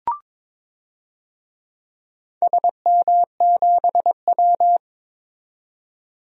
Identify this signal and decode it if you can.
Morse code, heard as a keyed tone, about 22 wpm, 700 Hz. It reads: SM7W
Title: DXCC-100 Call Signs - 4 Characters - 22wpm